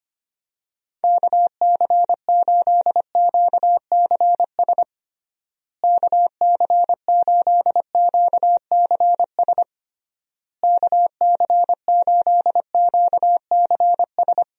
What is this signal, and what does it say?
Morse code, heard as a keyed tone, about 25 wpm, 700 Hz. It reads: KC8QCS KC8QCS KC8QCS
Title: USA Call Signs - 6 Characters - Repeated 3x - 25wpm